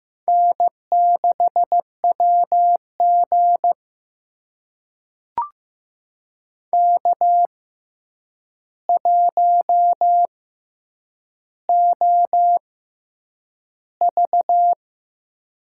Morse code, heard as a keyed tone, about 15 wpm, spaced as for 8 wpm, 700 Hz. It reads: N6WG K1OV
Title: USA Call Signs - 4 Characters - 6x Character Spacing - 15wpm